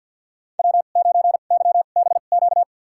Morse code, heard as a keyed tone, about 48 wpm, spaced as for 35 wpm, 690 Hz. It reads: W9YBQ
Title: USA Call Signs - 5 Characters - 2x Character Spacing - 50wpm